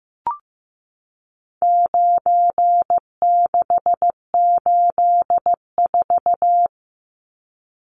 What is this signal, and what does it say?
Morse code, heard as a keyed tone, about 15 wpm, 700 Hz. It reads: 9684
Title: Numbers - 4-Digit 15wpm